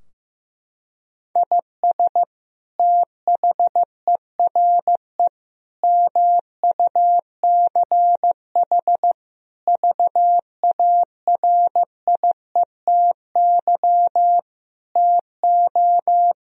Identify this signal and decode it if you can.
Morse code, heard as a keyed tone, about 15 wpm, 700 Hz. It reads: IS THERE MUCH VARIETY TO
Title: Sentences from Top 1000 Words - Review Part 4 - 15wpm